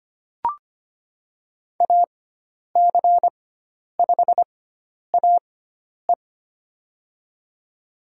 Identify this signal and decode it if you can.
Morse code, heard as a keyed tone, about 25 wpm, spaced as for 10 wpm, 700 Hz. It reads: AC5AE
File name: Lesson 16g - Letter c - Call Signs 25wpm